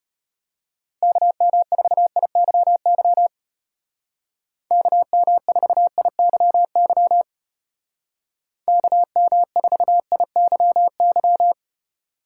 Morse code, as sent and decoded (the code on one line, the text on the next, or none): -.- -- ....- .. -.-- -.-- / -.- -- ....- .. -.-- -.-- / -.- -- ....- .. -.-- -.--
KM4IYY KM4IYY KM4IYY